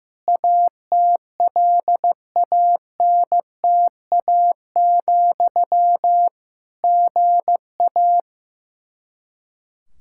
Morse code, heard as a keyed tone, about 15 wpm, 700 Hz. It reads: ATLANTA, GA
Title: Practice QSO - Exchange 1 - Beginning - 15wpm